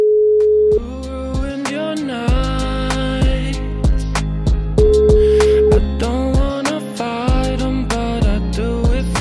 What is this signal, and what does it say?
Ringback tone (Europe, 425 Hz), with noise.